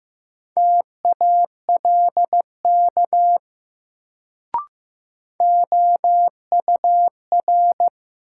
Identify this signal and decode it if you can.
Morse code, heard as a keyed tone, about 16 wpm, 700 Hz. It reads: TALK OUR
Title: Sets of 2 Words - New Word Drill - 15wpm